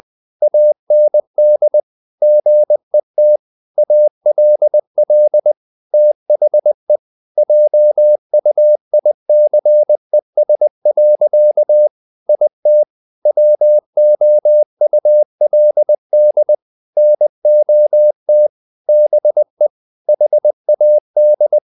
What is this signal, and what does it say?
Morse code, heard as a keyed tone, about 20 wpm, 600 Hz. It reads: AND GET ALL THE JUICES. IT WOULD NOT BE HAD
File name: the old man and the sea / the old man and the sea part 1 20wpm